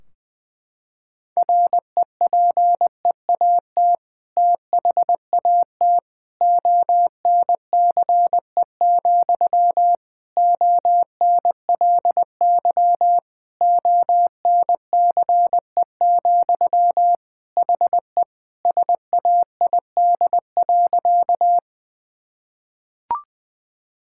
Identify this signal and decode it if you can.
Morse code, heard as a keyed tone, about 20 wpm, 700 Hz. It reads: REPEAT THAT ONCE, ONLY ONCE, HE SAID.
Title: Sentences from Top 1600 Words 20wpm